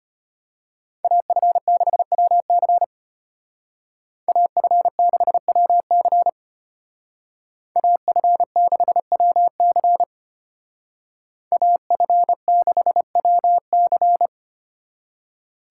Morse code, heard as a keyed tone, about 30 wpm, 700 Hz. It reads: AF6WC AF6WC AF6WC AF6WC
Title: Speed Racing - USA 2x2 Call Signs 25wpm